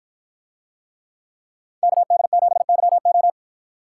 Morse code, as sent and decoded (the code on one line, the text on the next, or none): -.- -.. --... -.-- ---
KD7YO